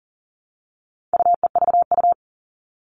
Morse code, heard as a keyed tone, about 40 wpm, 700 Hz. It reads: UEVU
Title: All Cut Numbers - 4-Digit 40wpm